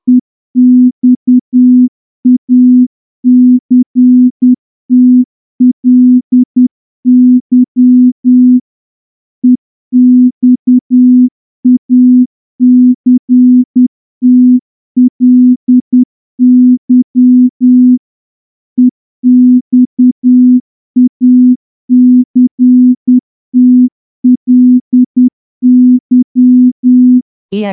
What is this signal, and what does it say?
Morse code, heard as a keyed tone, about 10 wpm, 250 Hz. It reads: EXACTLY EXACTLY EXACTLY
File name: morse-training-10WPM